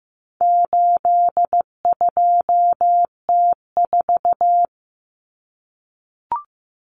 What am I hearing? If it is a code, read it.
Morse code, heard as a keyed tone, about 15 wpm, 700 Hz. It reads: 82T4
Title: Common Cut Numbers - 4-Digit 15wpm